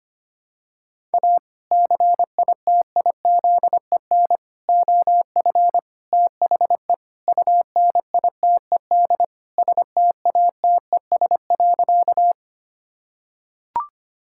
Morse code, heard as a keyed tone, about 25 wpm, 700 Hz. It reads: A CITIZEN OF THE UNITED STATES.
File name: practice / 25wpm / Sentences with 3-Word Phrases - 25wpm